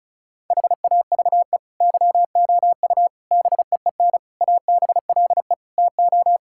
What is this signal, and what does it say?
Morse code, heard as a keyed tone, about 36 wpm, 700 Hz. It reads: HAVE YOU BEEN ABLE TO
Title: Sentences from Top 1000 Words - Review Part 6 - 35wpm